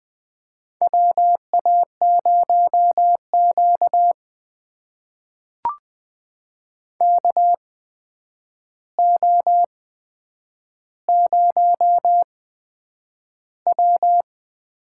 Morse code, heard as a keyed tone, about 20 wpm, spaced as for 9 wpm, 700 Hz. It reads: WA0Q KO0W
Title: USA Call Signs - 4 Characters - 8x Character Spacing - 20wpm